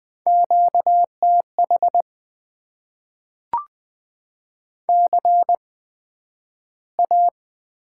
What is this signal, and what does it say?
Morse code, heard as a keyed tone, about 20 wpm, spaced as for 11 wpm, 700 Hz. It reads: QTH CA
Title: Top 90 QSO Elements - 8x Character Spacing - 20wpm